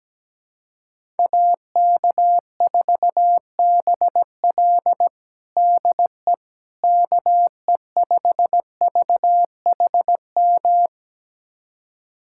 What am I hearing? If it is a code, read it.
Morse code, heard as a keyed tone, about 17 wpm, 700 Hz. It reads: AK4BL DE KE5VHM
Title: USA Callsign DE Callsign 17wpm